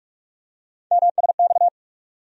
Morse code, heard as a keyed tone, about 46 wpm, 700 Hz. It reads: MSX
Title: USA Call Sign Postfixes - 3 Letters 45wpm